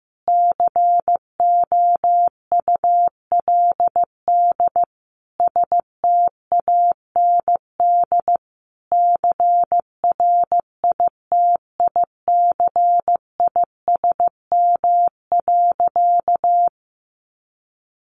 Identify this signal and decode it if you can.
Morse code, heard as a keyed tone, about 15 wpm, 700 Hz. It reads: COULD STAND CRITICISM.
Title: Sentences from Top 1300 Words 15wpm